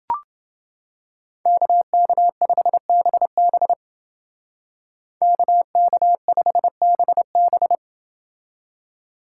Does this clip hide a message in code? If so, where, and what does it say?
Yes: Morse code from 1.5 to 7.8 s, heard as a keyed tone, about 29 wpm, 700 Hz. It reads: KK5BB KK5BB